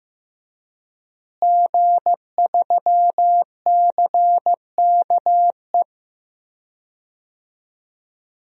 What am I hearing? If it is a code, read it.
Morse code, heard as a keyed tone, about 15 wpm, 700 Hz. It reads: G3CKE
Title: DXCC-100 Call Signs - 5 Characters - Repeated 2x - 15wpm